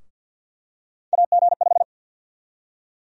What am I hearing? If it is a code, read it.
Morse code, heard as a keyed tone, about 51 wpm, 700 Hz. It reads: AG5
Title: USA Call Sign Prefixes - Repeated 2x - 50wpm